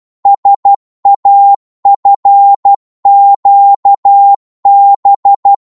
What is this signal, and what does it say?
Morse code, heard as a keyed tone, about 12 wpm, 800 Hz. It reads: SAFQB
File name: Morse / ltr-12wpm